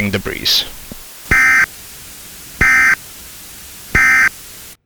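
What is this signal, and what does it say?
Emergency Alert System (SAME) tones, heard as harsh buzzing bursts, with speech. It is heard from 1.2 to 4.4 s.
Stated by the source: EAS alert tones used in a game map (TF2 custom map sound).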